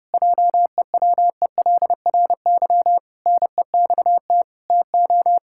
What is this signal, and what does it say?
Morse code, heard as a keyed tone, about 30 wpm, 700 Hz. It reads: JEWELRY NEXT TO
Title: Pangram Challenge v2 - 30wpm